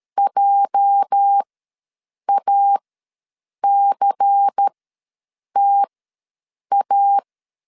Morse code, heard as a keyed tone, about 13 wpm, spaced as for 7 wpm, 800 Hz. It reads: JACTA
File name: LEZIONI / 8 YJ45 / 8 YJ45 ws 12wpm 3146